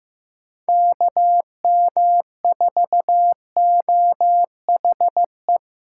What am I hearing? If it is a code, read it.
Morse code, heard as a keyed tone, about 18 wpm, 700 Hz. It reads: KM4OHE
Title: Speed Racing - USA 2x3 Call Signs 15wpm